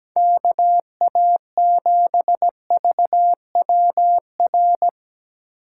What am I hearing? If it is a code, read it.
Morse code, heard as a keyed tone, about 17 wpm, 700 Hz. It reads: KA7VWR
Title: USA Call Signs - 6 Characters - 6x Character Spacing - 17wpm